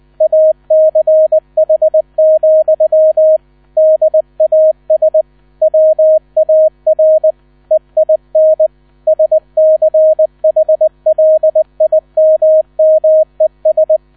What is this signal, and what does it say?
Morse code, heard as a keyed tone, about 20 wpm, 630 Hz, with radio noise. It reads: ACH, DAS WAR EIN SCHLIMMES
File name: 000 Vorwort 20wpm